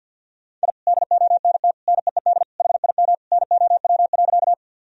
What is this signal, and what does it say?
Morse code, heard as a keyed tone, about 52 wpm, 700 Hz. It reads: I DONT NEED HIM NOW.
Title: Sentences from Top 1000 Words - Review Part 1 - 50wpm